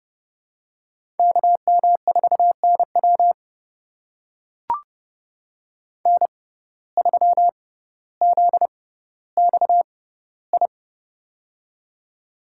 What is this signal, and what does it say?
Morse code, heard as a keyed tone, about 30 wpm, spaced as for 15 wpm, 700 Hz. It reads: KM4NW N3ZXI
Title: USA Call Signs - 5 Characters - 6x Character Spacing - 30wpm